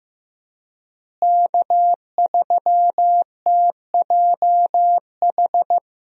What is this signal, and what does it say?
Morse code, heard as a keyed tone, about 15 wpm, 700 Hz. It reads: K3TJH
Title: USA Call Signs - 5 Characters - Repeated 3x - 15wpm